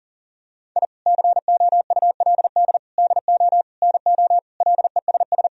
Morse code, heard as a keyed tone, about 40 wpm, 700 Hz. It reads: I COULD DO NO LESS
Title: Sentences from Top 1000 Words - Review Part 3 - 40wpm